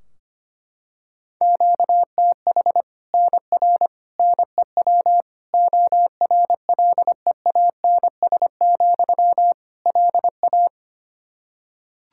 Morse code, heard as a keyed tone, about 25 wpm, 700 Hz. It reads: QTH NR NEW ORLEANS, LA
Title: Practice QSO - Exchange 1 - Beginning - 25wpm